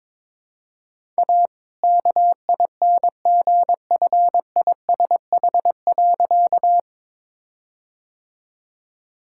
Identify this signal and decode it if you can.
Morse code, heard as a keyed tone, about 22 wpm, 700 Hz. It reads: A KINGFISH.
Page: morse-code-ninja-practice-22wpm